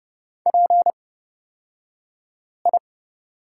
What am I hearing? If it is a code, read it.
Morse code, heard as a keyed tone, about 29 wpm, spaced as for 5 wpm, 700 Hz. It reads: PI
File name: Lesson 37d - Letter x - Two Characters 30wpm